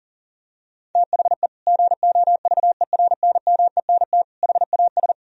Morse code, heard as a keyed tone, about 37 wpm, 700 Hz. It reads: THE GOVERNMENT HAS
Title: Sentences from Top 1000 Words - Review Part 3 - 40wpm